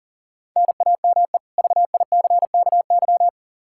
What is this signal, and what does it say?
Morse code, heard as a keyed tone, about 40 wpm, 700 Hz. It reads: NAME VICKY
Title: Practice QSO - Exchange 1 - Beginning - 40wpm